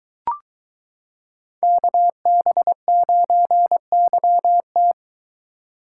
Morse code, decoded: KB9YT

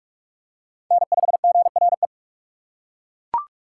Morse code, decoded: N5GRE

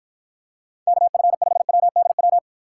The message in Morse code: KF5WDW

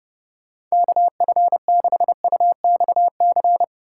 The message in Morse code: KF6UXC

